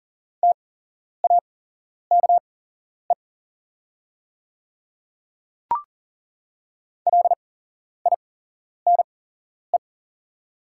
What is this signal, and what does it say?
Morse code, heard as a keyed tone, about 39 wpm, spaced as for 11 wpm, 690 Hz. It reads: TAKE LINE
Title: 8x Character Spacing - Top 500 Words - 40wpm